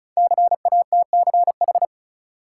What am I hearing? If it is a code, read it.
Morse code, heard as a keyed tone, about 34 wpm, 700 Hz. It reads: CATCH
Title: Sentences from Top 1500 Words - New Word Drill - 35wpm